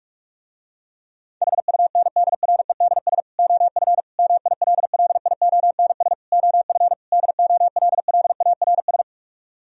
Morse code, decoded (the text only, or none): HUNDREDS OF MILLIONS OF DOLLARS